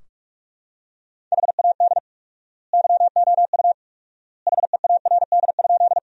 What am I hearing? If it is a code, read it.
Morse code, heard as a keyed tone, about 46 wpm, 700 Hz. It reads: HAD YOU HEARD?
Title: 4x Word-Spacing - Sentences from Top 400 Words - 45wpm